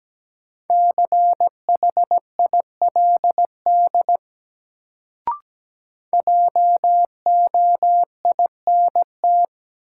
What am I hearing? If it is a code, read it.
Morse code, heard as a keyed tone, about 17 wpm, 700 Hz. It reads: CHILD JOINT
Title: Sets of 5 Words Encore - New Word Drill - 17wpm